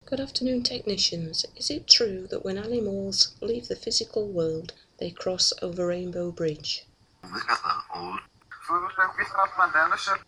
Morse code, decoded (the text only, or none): none